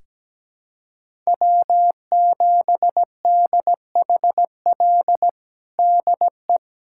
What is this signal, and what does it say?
Morse code, heard as a keyed tone, about 17 wpm, 700 Hz. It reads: W7DHL DE
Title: USA Callsign DE Callsign - 5 Characters - 17wpm